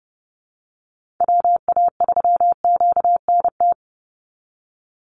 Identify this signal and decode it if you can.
Morse code, heard as a keyed tone, about 30 wpm, 700 Hz. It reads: WA3QNT
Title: Call Signs Repeated 2x - 30wpm